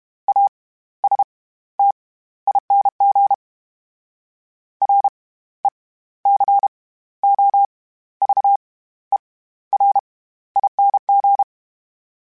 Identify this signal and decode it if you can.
Morse code, heard as a keyed tone, about 32 wpm, spaced as for 15 wpm, 800 Hz. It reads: ASTING RECOVERING
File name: Special INGs / Special INGs-mp3s / SlowToFaster ing @30